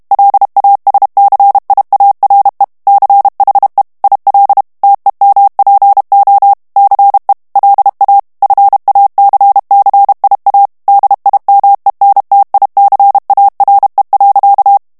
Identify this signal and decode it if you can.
Morse code, heard as a keyed tone, about 32 wpm, 800 Hz. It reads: LASCIARE CHE IL TEMPO CE LA FACCIA DIMENTICARE.
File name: ITALIEN texte clair 1 vitesse 30 mots minute